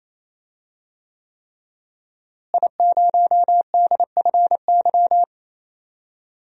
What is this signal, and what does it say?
Morse code, heard as a keyed tone, about 28 wpm, 700 Hz. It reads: I0DFY
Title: DXCC-100 Call Signs - 5 Characters - Repeated 3x - 28wpm